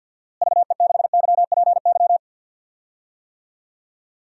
Morse code, heard as a keyed tone, about 50 wpm, 700 Hz. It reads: VE6YPY
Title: DXCC-100 Call Signs - 6 Characters - Repeated 2x - 50wpm